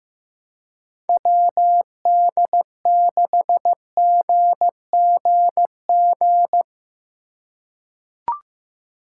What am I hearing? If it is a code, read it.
Morse code, heard as a keyed tone, about 15 wpm, 700 Hz. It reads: WD6GGG
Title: USA Call Signs - 6 Characters - 2x Character Spacing - 15wpm